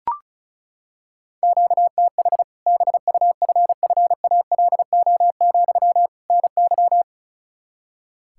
Morse code, heard as a keyed tone, about 34 wpm, 700 Hz. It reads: QTH BUFFALO, NY
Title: Practice QSO - Exchange 1 - Beginning - 35wpm